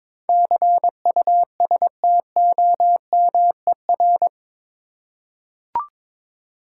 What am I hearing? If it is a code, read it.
Morse code, heard as a keyed tone, about 22 wpm, 700 Hz. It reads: CUSTOMER
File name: practice / 22wpm / Eight Letter Words - 22wpm